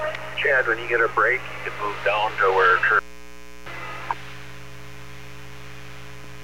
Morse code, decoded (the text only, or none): none